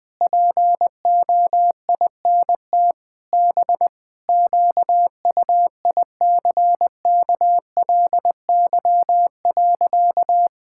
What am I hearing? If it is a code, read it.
Morse code, heard as a keyed tone, about 20 wpm, 700 Hz. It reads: POINT B QUICKLY.
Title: Sentences with 7-Word Phrases - 20wpm